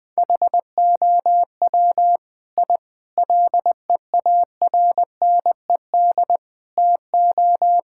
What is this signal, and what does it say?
Morse code, heard as a keyed tone, about 21 wpm, 700 Hz. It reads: HOW I LEARNED TO
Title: Sets of 8 Words - 20wpm